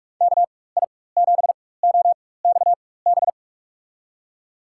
Morse code, heard as a keyed tone, about 43 wpm, spaced as for 21 wpm, 690 Hz. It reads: KI7OXB
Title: USA Call Signs - 4x Character Spacing - 45wpm